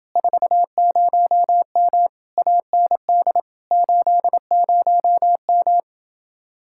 Morse code, heard as a keyed tone, about 28 wpm, 700 Hz. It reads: 40M AND 80M